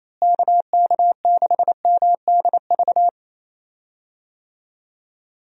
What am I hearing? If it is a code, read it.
Morse code, heard as a keyed tone, about 28 wpm, 700 Hz. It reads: KK6MDV
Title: USA Call Signs - 6 Characters - Repeated 3x - 28wpm